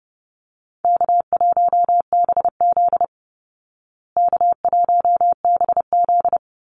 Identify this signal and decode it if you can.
Morse code, heard as a keyed tone, about 30 wpm, 700 Hz. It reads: K1BZ K1BZ
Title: Call Signs Repeated 3x - 30wpm